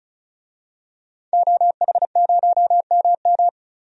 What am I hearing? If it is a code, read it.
Morse code, heard as a keyed tone, about 36 wpm, 700 Hz. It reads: OH0MM
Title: DXCC-100 Call Signs - 5 Characters - 35wpm